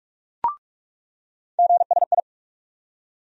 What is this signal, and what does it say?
Morse code, heard as a keyed tone, about 46 wpm, 700 Hz. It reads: GSI
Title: USA Call Sign Postfixes - 3 Letters 45wpm